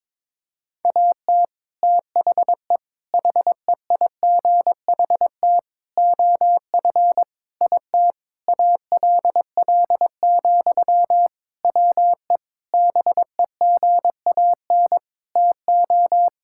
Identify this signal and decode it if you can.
Morse code, heard as a keyed tone, about 22 wpm, 700 Hz. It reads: AT THE HEIGHT OF IT ALL, WE BEGAN TO
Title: Sentences from Top 1600 Words 22wpm